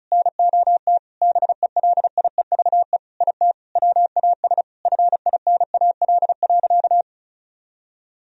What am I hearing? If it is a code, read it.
Morse code, heard as a keyed tone, about 34 wpm, 700 Hz. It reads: NOT BELIEVE IT WAS FINAL.
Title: Sentences from Top 1000 Words - Review Part 2 - 35wpm